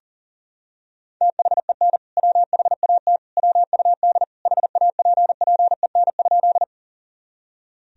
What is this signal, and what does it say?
Morse code, heard as a keyed tone, about 40 wpm, 700 Hz. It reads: THEN WHAT WUD HAPPEN?
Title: Sentences from Top 400 Words and Abbreviations 40wpm